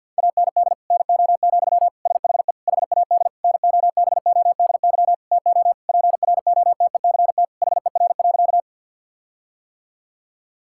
Morse code, heard as a keyed tone, about 46 wpm, 700 Hz. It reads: AND NO, SHE HAD NOBODY TO PROTECT HER.
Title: Sentences from Top 1600 Words 50wpm